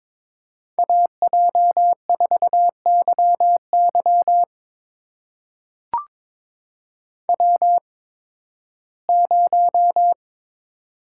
Morse code, heard as a keyed tone, about 22 wpm, spaced as for 14 wpm, 700 Hz. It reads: AJ4YY W0
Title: USA Call Signs - 8x Character Spacing - 22wpm